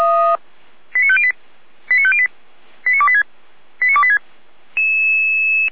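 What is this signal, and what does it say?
Five-tone selective call (ZVEI), heard as quick runs of five tones, tones of about 70 ms. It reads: 89508 89508 89286 89286
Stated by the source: Five-tone ZVEI selective call received on SDR, fire brigade Orlamünde (feuerwehr-orlamuende).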